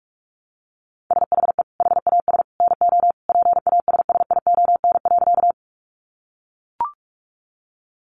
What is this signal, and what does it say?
Morse code, heard as a keyed tone, about 44 wpm, 700 Hz. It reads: SHE HAS NO PASSION.
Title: Sentences from Top 1300 Words 45wpm